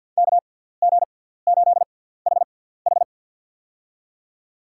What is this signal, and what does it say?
Morse code, heard as a keyed tone, about 53 wpm, spaced as for 17 wpm, 700 Hz. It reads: KG8HH
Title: USA Call Signs - 5 Characters - 6x Character Spacing - 50wpm